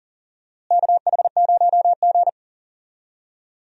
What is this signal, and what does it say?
Morse code, heard as a keyed tone, about 40 wpm, 700 Hz. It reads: KH0G
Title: USA Call Signs - 4 Characters - Repeated 3x - 40wpm